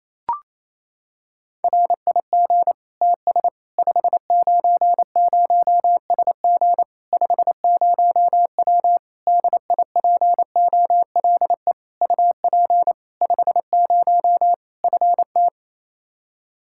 Morse code, heard as a keyed tone, about 28 wpm, 700 Hz. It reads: RIG TS 590SG 50W DIPOLE UP 50 FT
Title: Practice QSO - Exchange 2 - Info - 28wpm